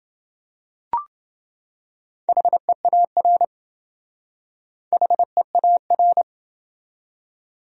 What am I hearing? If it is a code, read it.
Morse code, heard as a keyed tone, about 28 wpm, 700 Hz. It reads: HEAR HEAR